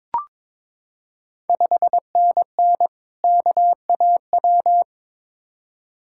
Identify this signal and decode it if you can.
Morse code, heard as a keyed tone, about 22 wpm, 700 Hz. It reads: 5NN KAW
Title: Ontario QSO Party Exchange 22wpm